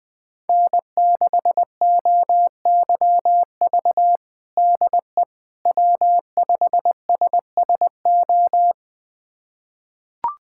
Morse code, heard as a keyed tone, about 20 wpm, 700 Hz. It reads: N6OYV DE W5SSO
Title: USA Callsign DE Callsign - 5 Characters - 20wpm